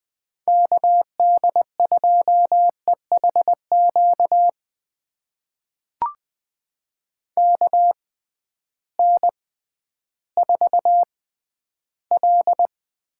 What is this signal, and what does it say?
Morse code, heard as a keyed tone, about 20 wpm, spaced as for 12 wpm, 700 Hz. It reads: KD2EHQ KN4L
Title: USA Call Signs - 6 Characters - 6x Character Spacing - 20wpm